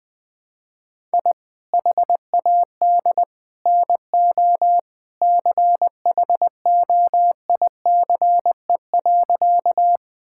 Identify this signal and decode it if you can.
Morse code, heard as a keyed tone, about 20 wpm, 700 Hz. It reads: I HAD NO CHOICE.
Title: Sentences from Top 1000 Words - Review Part 6 - 20wpm